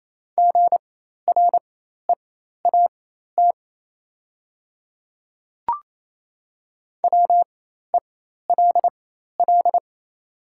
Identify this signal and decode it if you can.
Morse code, heard as a keyed tone, about 27 wpm, spaced as for 13 wpm, 700 Hz. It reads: GREAT WELL
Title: 4x Character Spacing - Top 300 Words - 28wpm